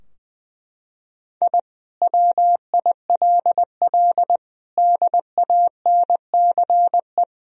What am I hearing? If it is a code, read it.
Morse code, heard as a keyed tone, about 20 wpm, 700 Hz. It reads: I WILL DANCE
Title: Sentences from Top 1000 Words - Review Part 4 - 20wpm